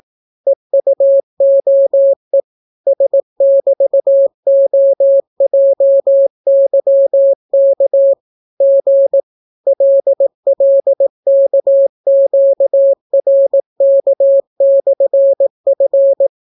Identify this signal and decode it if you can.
Morse code, heard as a keyed tone, about 18 wpm, 550 Hz. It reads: EUOE S<BT>OJYK G LLKQRK/F